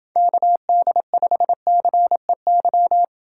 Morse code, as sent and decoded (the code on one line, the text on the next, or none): -.- -.. ..... -.-. -.--
KD5CEY